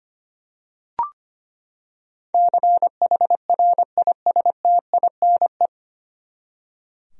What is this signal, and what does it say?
Morse code, heard as a keyed tone, about 25 wpm, 700 Hz. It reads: CHRISTINE
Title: USA Common Names - 25wpm